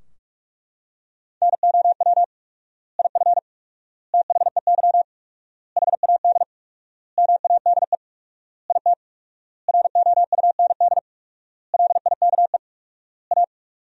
Morse code, heard as a keyed tone, about 42 wpm, 700 Hz. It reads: NOW IF THEY HAD MADE IT ROUND LIKE A
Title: 4x Word-Spacing - Sentences from Top 500 Words - 45wpm